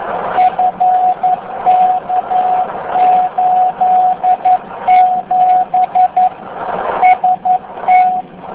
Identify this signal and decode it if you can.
Morse code, heard as a keyed tone, about 11 wpm, 740 Hz. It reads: FK87 ST